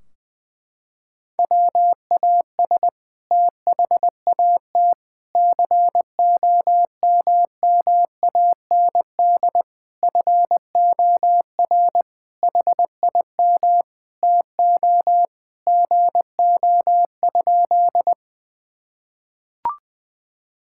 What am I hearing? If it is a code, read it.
Morse code, heard as a keyed tone, about 20 wpm, 700 Hz. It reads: WAS THAT COMMAND FOR HIM TO GO?
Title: Sentences from Top 1600 Words 20wpm